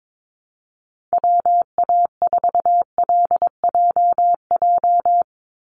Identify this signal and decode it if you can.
Morse code, heard as a keyed tone, about 22 wpm, 700 Hz. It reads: WA4LJJ
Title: Call Signs Repeated 3x - 22wpm